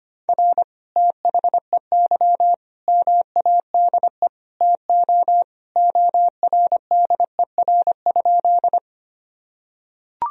Morse code, heard as a keyed tone, about 25 wpm, 700 Hz. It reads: R THEY MADE TO ORDER?
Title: Sentences from Top 400 Words and Abbreviations 25wpm